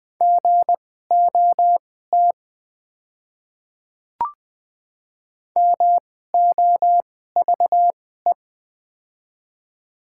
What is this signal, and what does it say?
Morse code, heard as a keyed tone, about 21 wpm, spaced as for 15 wpm, 700 Hz. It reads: GOT MOVE
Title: 2x Character Spacing - Top 500 Words - 20wpm